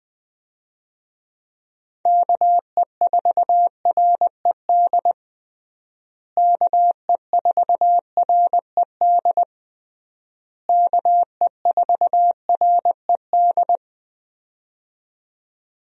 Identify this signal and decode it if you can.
Morse code, heard as a keyed tone, about 20 wpm, 700 Hz. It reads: KE4RED KE4RED KE4RED
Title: USA Call Signs - 6 Characters - Repeated 3x - 20wpm